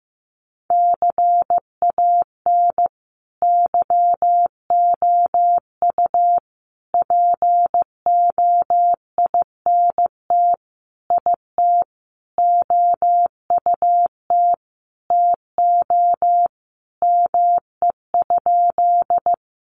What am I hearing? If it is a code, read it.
Morse code, heard as a keyed tone, about 15 wpm, 700 Hz. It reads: CAN YOU POINT IT OUT TO ME?
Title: Sentences from Top 200 Words 15wpm